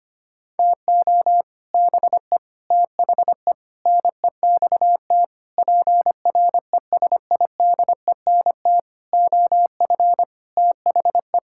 Morse code, heard as a keyed tone, about 25 wpm, 700 Hz. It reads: TO BE THE NEXT PRESIDENT OF THE
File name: Sets of 9 Words - 25wpm